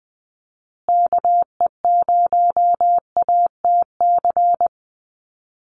Morse code, heard as a keyed tone, about 20 wpm, 700 Hz. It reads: KE0ATC
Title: USA Call Signs Repeated 3x 20wpm